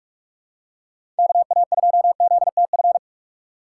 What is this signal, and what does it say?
Morse code, heard as a keyed tone, about 43 wpm, 700 Hz. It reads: KA2ZTF
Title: USA Call Signs - 6 Characters - 8x Character Spacing - 45wpm